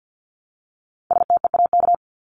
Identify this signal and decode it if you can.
Morse code, heard as a keyed tone, about 50 wpm, 700 Hz. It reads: STEAK
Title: Top 1251-1500 Words 50wpm